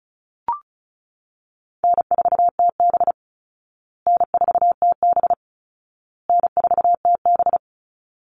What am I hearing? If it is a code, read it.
Morse code, heard as a keyed tone, about 34 wpm, 700 Hz. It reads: N4TB N4TB N4TB
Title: Easy Call Signs Repeated 3x - 35wpm